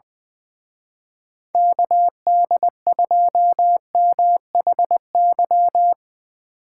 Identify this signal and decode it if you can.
Morse code, heard as a keyed tone, about 22 wpm, 700 Hz. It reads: KD2MHY
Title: Speed Racing - USA 2x3 Call Signs 20wpm